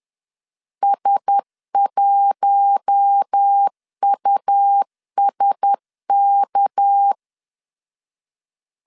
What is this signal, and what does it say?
Morse code, heard as a keyed tone, about 10 wpm, 800 Hz. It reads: S1USK
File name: CW / DET / 1 9 1500C S B 10wpm 4439